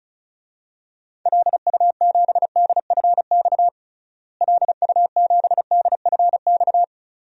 Morse code, heard as a keyed tone, about 36 wpm, 700 Hz. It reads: LU7DFX LU7DFX
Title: DXCC-100 Call Signs - 6 Characters - Repeated 2x - 35wpm